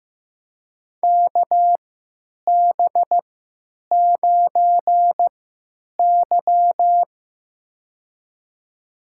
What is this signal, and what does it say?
Morse code, heard as a keyed tone, about 15 wpm, spaced as for 8 wpm, 700 Hz. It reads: KB9Y
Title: USA Call Signs - 4 Characters - 3x Character Spacing - 15wpm